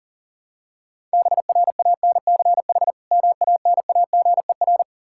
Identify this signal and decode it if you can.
Morse code, heard as a keyed tone, about 40 wpm, 700 Hz. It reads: BRANCH MANAGER
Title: Practice QSO - Exchange 3 - More Info - 40wpm